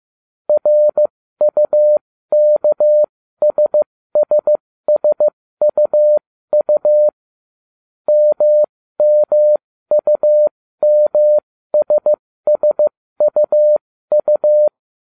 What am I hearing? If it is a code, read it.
Morse code, heard as a keyed tone, about 15 wpm, spaced as for 13 wpm, 600 Hz. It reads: RUKSSSUU MMUMSSUU